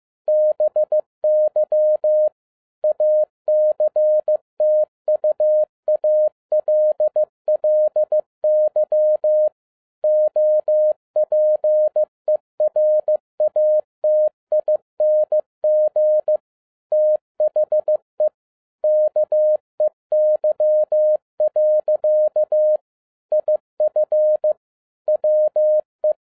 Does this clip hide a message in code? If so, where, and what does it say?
Yes: Morse code from 0.3 to 26.1 s, heard as a keyed tone, about 15 wpm, 610 Hz. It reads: BY ACTUALLY OPERATING THE KEY. IF WE